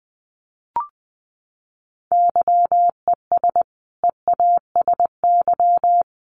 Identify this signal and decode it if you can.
Morse code, heard as a keyed tone, about 20 wpm, 700 Hz. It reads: YES EASY
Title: Sentences from Top 700 Words 20wpm